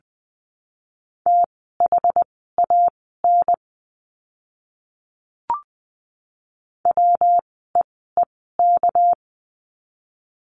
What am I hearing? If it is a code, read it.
Morse code, heard as a keyed tone, about 20 wpm, spaced as for 14 wpm, 700 Hz. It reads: THAN WEEK
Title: 244 Lesson 39f - Letter z - Words 20wpm